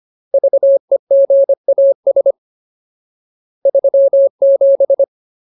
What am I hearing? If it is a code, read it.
Morse code, heard as a keyed tone, about 25 wpm, 550 Hz. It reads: VEGAS 37
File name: Ultimate Word List / 25WPM Ultimate